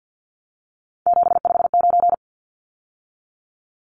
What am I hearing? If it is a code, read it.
Morse code, heard as a keyed tone, about 50 wpm, 700 Hz. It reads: 759
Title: Numbers - 3-Digit 50wpm